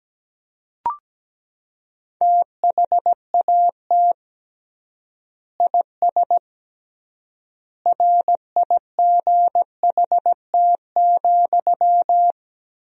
Morse code, heard as a keyed tone, about 17 wpm, 700 Hz. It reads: THAT IS RIGHT,
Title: 3x Word-Spacing - Sentences from Top 200 Words - 17wpm